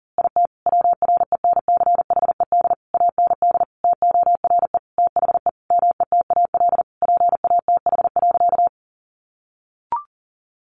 Morse code, heard as a keyed tone, about 40 wpm, 700 Hz. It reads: IT WRENCHED AND TORE THE METAL PATH.
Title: A Sound Of Thunder 40wpm